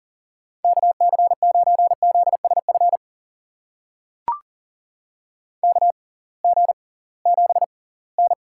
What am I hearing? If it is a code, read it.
Morse code, heard as a keyed tone, about 40 wpm, spaced as for 23 wpm, 700 Hz. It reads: KC9ZSF KG7N
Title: USA Call Signs - 6x Character Spacing - 40wpm